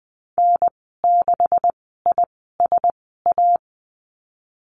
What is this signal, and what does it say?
Morse code, heard as a keyed tone, about 20 wpm, spaced as for 15 wpm, 700 Hz. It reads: N6ISA
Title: 128 Lesson 21g - Number 6 - Call Signs 20wpm